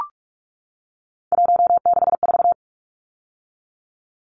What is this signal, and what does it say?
Morse code, heard as a keyed tone, about 44 wpm, 700 Hz. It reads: 164
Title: Numbers - 3-Digit 45wpm